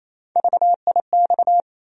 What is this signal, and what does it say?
Morse code, heard as a keyed tone, about 28 wpm, 700 Hz. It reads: VIX
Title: USA Call Sign Postfixes - 3 Letters 28wpm